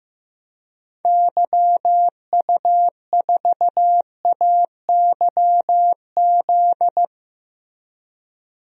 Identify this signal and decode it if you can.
Morse code, heard as a keyed tone, about 15 wpm, 700 Hz. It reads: YU4AYZ
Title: DXCC-100 Call Signs - 6 Characters - Repeated 3x - 15wpm